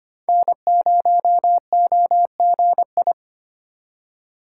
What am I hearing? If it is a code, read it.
Morse code, heard as a keyed tone, about 25 wpm, 700 Hz. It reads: N0OGI
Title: USA Call Signs - 5 Characters - 3x Character Spacing - 25wpm